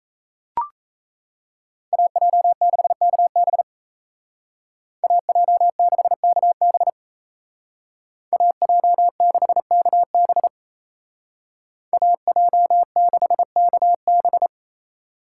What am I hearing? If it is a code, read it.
Morse code, heard as a keyed tone, about 34 wpm, 700 Hz. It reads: AJ6KB AJ6KB AJ6KB AJ6KB
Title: Speed Racing - USA 2x2 Call Signs 28wpm